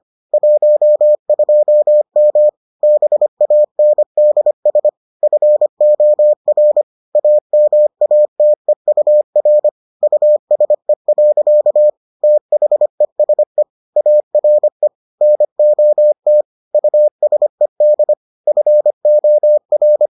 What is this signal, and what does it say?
Morse code, heard as a keyed tone, about 25 wpm, 600 Hz. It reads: 12M BANDS FOR AMATEUR USE. THESE ARE NOT USED FOR